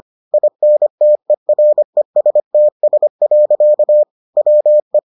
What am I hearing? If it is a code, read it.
Morse code, heard as a keyed tone, about 25 wpm, 600 Hz. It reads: INTERESTS. WE